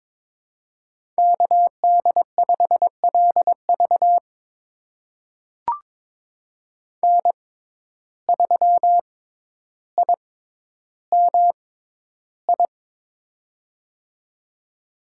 Morse code, heard as a keyed tone, about 22 wpm, spaced as for 11 wpm, 700 Hz. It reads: KD5LV N3IMI
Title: USA Call Signs - 5 Characters - 6x Character Spacing - 22wpm